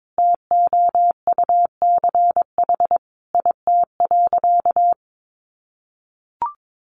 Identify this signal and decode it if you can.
Morse code, heard as a keyed tone, about 22 wpm, 700 Hz. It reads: TOUCH IT.
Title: Sentences from Top 1200 Words 22wpm